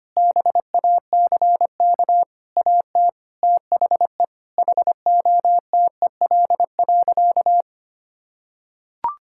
Morse code, heard as a keyed tone, about 25 wpm, 700 Hz. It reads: BACK AT THE HOTEL.
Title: Sentences from Top 1600 Words 25wpm